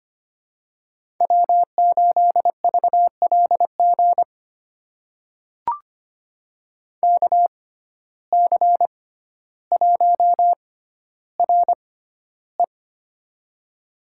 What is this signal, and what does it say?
Morse code, heard as a keyed tone, about 25 wpm, spaced as for 13 wpm, 700 Hz. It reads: W8VLG KC1RE
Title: USA Call Signs - 5 Characters - 6x Character Spacing - 25wpm